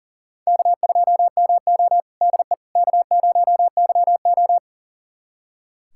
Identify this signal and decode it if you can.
Morse code, heard as a keyed tone, about 39 wpm, 700 Hz. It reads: K2MO DE K0YO
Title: USA Callsign DE Callsign - 4 Characters - 40wpm